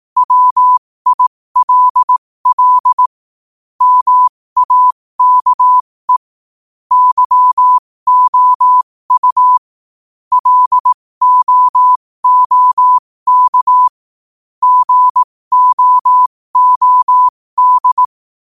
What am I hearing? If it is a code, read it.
Morse code, heard as a keyed tone, about 18 wpm, spaced as for 15 wpm, 1000 Hz. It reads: WILL MAKE YOU LOOK GOOD
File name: qotdfiles / 15wpm